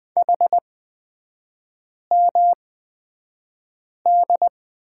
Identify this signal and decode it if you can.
Morse code, heard as a keyed tone, about 20 wpm, spaced as for 5 wpm, 700 Hz. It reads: HMD
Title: 255 Lesson 41e - Period - Three Characters 20wpm